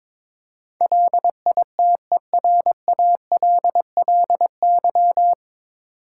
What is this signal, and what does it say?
Morse code, heard as a keyed tone, about 23 wpm, 700 Hz. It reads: LITERALLY
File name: 9 Letter Words - First 3 Letters Given - 22wpm